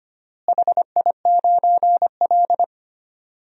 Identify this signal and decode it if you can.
Morse code, heard as a keyed tone, about 25 wpm, 700 Hz. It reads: HI9L
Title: DXCC-100 Call Signs - 4 Characters - Repeated 2x - 25wpm